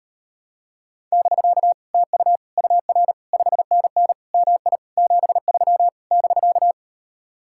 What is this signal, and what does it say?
Morse code, heard as a keyed tone, about 38 wpm, 700 Hz. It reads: <BK> TU UR 5NN MI 73 <BK>